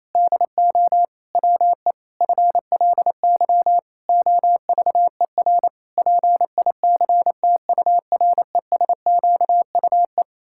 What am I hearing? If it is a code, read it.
Morse code, heard as a keyed tone, about 28 wpm, 700 Hz. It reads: DO WE FLY OVER PICTURESQUE
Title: Pangram Challenge v2 - 28wpm